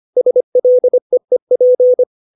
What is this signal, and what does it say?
Morse code, heard as a keyed tone, about 25 wpm, 500 Hz. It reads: SLEEP
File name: 632 words / 632 Words 25WPM